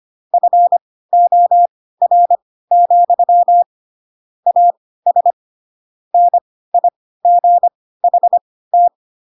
Morse code, heard as a keyed tone, about 25 wpm, spaced as for 16 wpm, 700 Hz. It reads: FOR, AS NIGHT
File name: Pinocchio / Pinocchio ch5 10 25 ik1hge